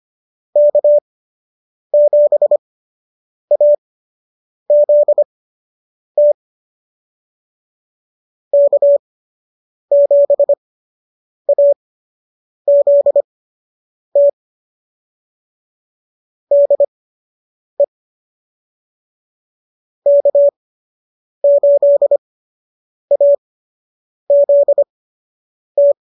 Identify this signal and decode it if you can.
Morse code, heard as a keyed tone, about 25 wpm, spaced as for 8 wpm, 590 Hz. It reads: K7AZT K7AZT DE K8AZT
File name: qso006 08